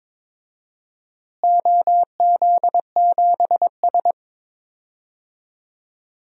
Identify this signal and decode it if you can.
Morse code, heard as a keyed tone, about 22 wpm, 700 Hz. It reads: OZ7S
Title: DXCC-100 Call Signs - 4 Characters - Repeated 2x - 22wpm